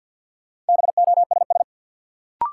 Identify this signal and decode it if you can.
Morse code, heard as a keyed tone, about 50 wpm, 700 Hz. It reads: BOSS
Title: Sentences from Top 700 Words - New Word Drill - 50wpm